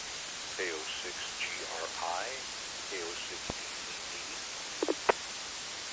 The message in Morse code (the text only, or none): none